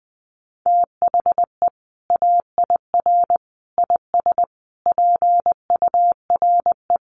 Morse code, heard as a keyed tone, about 20 wpm, 700 Hz. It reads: THE AIR IS PURE
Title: Sentences from Top 1500 Words 20wpm